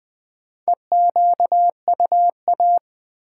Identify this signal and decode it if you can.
Morse code, heard as a keyed tone, about 20 wpm, 700 Hz. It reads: EQUA
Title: Build A Word - 2 Letters At A Time - 8-Letter Words - 20wpm